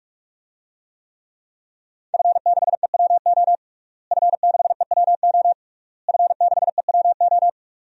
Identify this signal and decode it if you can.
Morse code, heard as a keyed tone, about 46 wpm, 700 Hz. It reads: F6EWO F6EWO F6EWO
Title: DXCC-100 Call Signs - 5 Characters - Repeated 3x - 45wpm